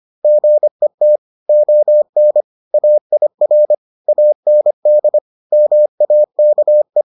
Morse code, heard as a keyed tone, about 25 wpm, 600 Hz. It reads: GET ON AIR AND MAKE